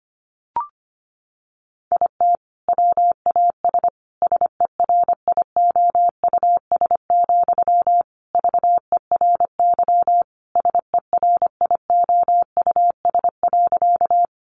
Morse code, heard as a keyed tone, about 25 wpm, 700 Hz. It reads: IT WAS SERIOUS, VERY SERIOUS.
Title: Sentences from Top 1100 Words 25wpm